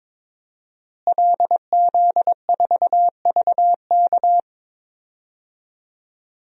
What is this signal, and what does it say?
Morse code, heard as a keyed tone, about 22 wpm, 700 Hz. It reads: LZ4VK
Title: DXCC-100 Call Signs - 5 Characters - Repeated 3x - 22wpm